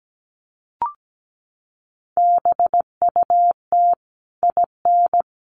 Morse code, heard as a keyed tone, about 17 wpm, 700 Hz. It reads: BUT IN
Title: Sentences from Top 1200 Words 17wpm